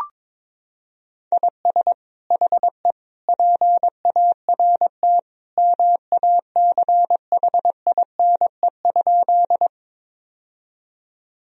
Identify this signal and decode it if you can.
Morse code, heard as a keyed tone, about 22 wpm, 700 Hz. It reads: IS HE PART MACHINE?
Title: Sentences from Top 1000 Words - Review Part 2 - 22wpm